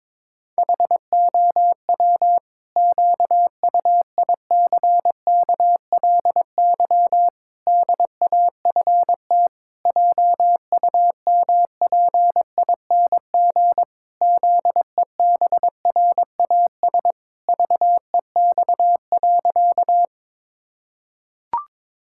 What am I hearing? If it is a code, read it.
Morse code, heard as a keyed tone, about 22 wpm, 700 Hz. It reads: HOW QUICKLY DAFT JUMPING ZEBRAS VEX.
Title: Pangram Challenge v2 - 22wpm